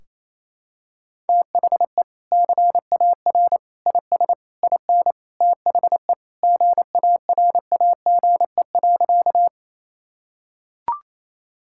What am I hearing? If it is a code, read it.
Morse code, heard as a keyed tone, about 28 wpm, 700 Hz. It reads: THE CAR IS IN THE GARAGE.
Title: Sentences from Top 1600 Words 28wpm